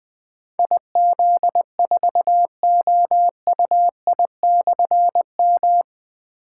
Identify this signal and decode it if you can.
Morse code, heard as a keyed tone, about 20 wpm, 700 Hz. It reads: IZ4OUI/M
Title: Difficult Call Signs 20wpm